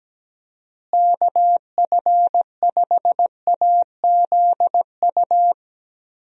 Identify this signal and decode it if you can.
Morse code, heard as a keyed tone, about 17 wpm, 700 Hz. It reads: KF5AZU
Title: USA Call Signs - 6 Characters - Repeated 2x - 17wpm